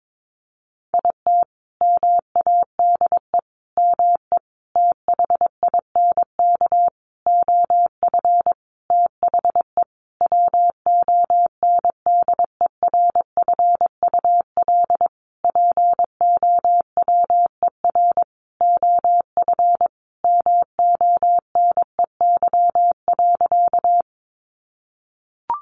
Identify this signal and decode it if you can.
Morse code, heard as a keyed tone, about 22 wpm, 700 Hz. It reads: IT MADE ME THINK OF THE WONDERFUL POWER OF MONEY.
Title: Sentences from Top 1100 Words 22wpm